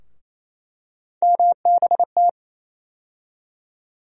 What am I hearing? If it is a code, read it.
Morse code, heard as a keyed tone, about 28 wpm, 700 Hz. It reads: MBT
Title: USA Call Sign Postfixes - 3 Letters 28wpm